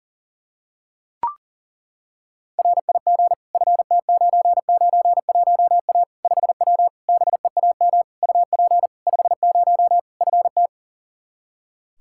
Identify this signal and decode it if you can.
Morse code, heard as a keyed tone, about 37 wpm, 700 Hz. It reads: RIG FT991A 5W BEAM UP 50 FT
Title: Practice QSO - Exchange 2 - Info - 40wpm